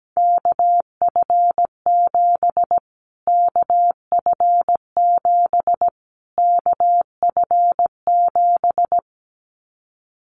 Morse code, heard as a keyed tone, about 17 wpm, 700 Hz. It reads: KF7 KF7 KF7
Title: Callsign Prefixes for Top 25 DXCC - Repeated 3x - 17wpm